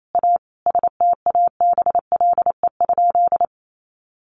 Morse code, heard as a keyed tone, about 28 wpm, 700 Hz. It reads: A STABLE?
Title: Sentences from Top 1500 Words 28wpm